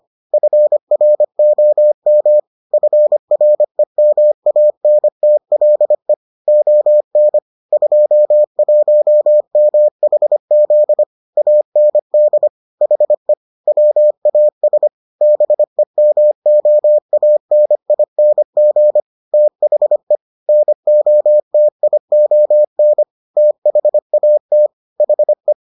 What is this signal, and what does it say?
Morse code, heard as a keyed tone, about 25 wpm, 600 Hz. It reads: FROM FREMANTLE ON 21MHZ AND HE WAS BEMOANING THE NOTION THAT HE